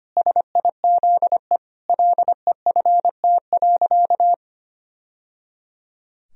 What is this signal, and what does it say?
Morse code, heard as a keyed tone, about 25 wpm, 700 Hz. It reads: SIZE LEFT.
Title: Sentences from Top 1000 Words - Review Part 2 - 25wpm